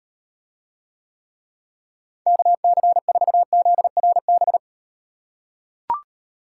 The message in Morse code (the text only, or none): KC4ZRB